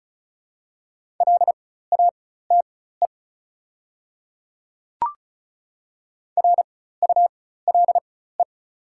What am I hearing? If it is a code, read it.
Morse code, heard as a keyed tone, about 36 wpm, spaced as for 16 wpm, 690 Hz. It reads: LATE RULE
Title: 4x Character Spacing - Top 500 Words - 35wpm